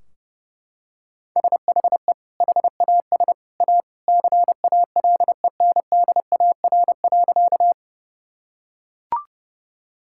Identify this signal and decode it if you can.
Morse code, heard as a keyed tone, about 30 wpm, 700 Hz. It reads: SHE HAS A CALENDAR.
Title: Sentences from Top 1600 Words 30wpm